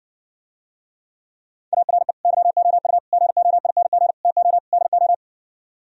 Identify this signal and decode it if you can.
Morse code, heard as a keyed tone, about 66 wpm, 700 Hz. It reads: ARE YOU GOING TO DO